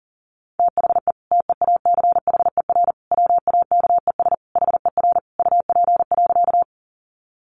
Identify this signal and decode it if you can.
Morse code, heard as a keyed tone, about 37 wpm, 700 Hz. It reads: THE TEACHER WAKES HER UP.
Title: Easy Phrases 40wpm